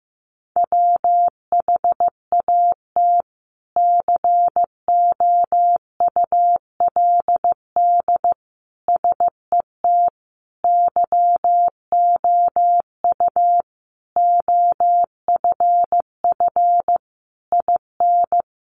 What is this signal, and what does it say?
Morse code, heard as a keyed tone, about 15 wpm, 700 Hz. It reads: WHAT COULD SET YOU OFF IN
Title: Sentences from Top 200 Words 15wpm